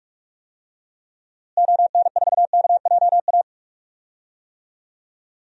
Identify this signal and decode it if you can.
Morse code, heard as a keyed tone, about 46 wpm, 700 Hz. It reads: ON4KJA
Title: DXCC-100 Call Signs - 6 Characters - Repeated 2x - 45wpm